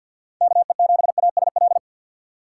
Morse code, heard as a keyed tone, about 61 wpm, 700 Hz. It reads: KE7ASL
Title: Speed Racing - USA 2x3 Call Signs 50wpm